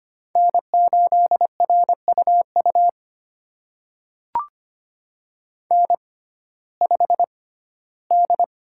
Morse code, heard as a keyed tone, about 25 wpm, spaced as for 15 wpm, 700 Hz. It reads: N8RUU N5D
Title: USA Call Signs - 6x Character Spacing - 25wpm